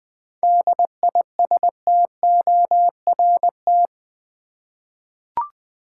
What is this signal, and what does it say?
Morse code, heard as a keyed tone, about 20 wpm, 700 Hz. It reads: DISTORT
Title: 7 Letter Words - First 3 Letters Given - 20wpm